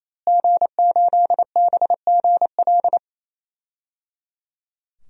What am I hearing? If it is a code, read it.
Morse code, heard as a keyed tone, about 27 wpm, 700 Hz. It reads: G8BGL
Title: DXCC-100 Call Signs - 5 Characters - Repeated 3x - 28wpm